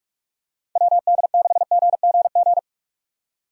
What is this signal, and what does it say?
Morse code, heard as a keyed tone, about 46 wpm, 700 Hz. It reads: WD6GGG
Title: USA Call Signs - 6 Characters - 45wpm